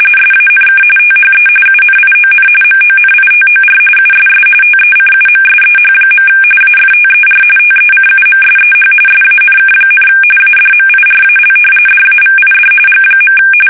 RTTY, about 45.45 baud. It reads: SANK THE UNION MONITOR USS TECUMSEH. AFTER TECUMSEH SANK, ADMIRAL DAVID G.
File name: fsk850 / usnavy3